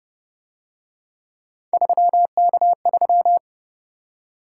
Morse code, heard as a keyed tone, about 30 wpm, 700 Hz. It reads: 3K3